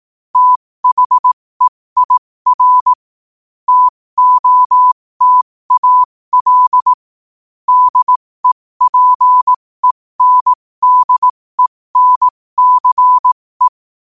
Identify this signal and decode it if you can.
Morse code, heard as a keyed tone, about 18 wpm, spaced as for 15 wpm, 1000 Hz. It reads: THEIR TOTAL DEPENDENCE
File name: qotdfiles / 15wpm